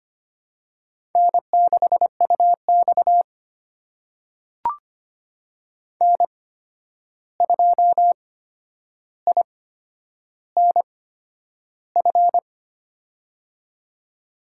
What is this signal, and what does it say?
Morse code, heard as a keyed tone, about 25 wpm, spaced as for 10 wpm, 700 Hz. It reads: N6UX N2INF
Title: USA Call Signs - 8x Character Spacing - 25wpm